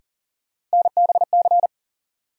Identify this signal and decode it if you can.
Morse code, heard as a keyed tone, about 40 wpm, 700 Hz. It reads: NBC
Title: Sets of 5 Words Encore - New Word Drill - 40wpm